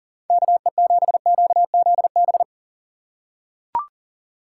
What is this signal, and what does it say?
Morse code, heard as a keyed tone, about 40 wpm, 700 Hz. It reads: KE7QZB
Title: USA Call Signs - 6 Characters - 2x Character Spacing - 40wpm